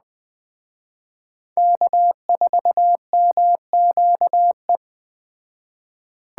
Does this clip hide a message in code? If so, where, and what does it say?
Yes: Morse code from 1.6 to 4.8 s, heard as a keyed tone, about 21 wpm, 700 Hz. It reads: K4MQE